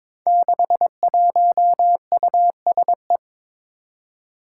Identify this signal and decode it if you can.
Morse code, heard as a keyed tone, about 22 wpm, 700 Hz. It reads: 61USE
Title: Five Challenging Letter-Number Combinations 22wpm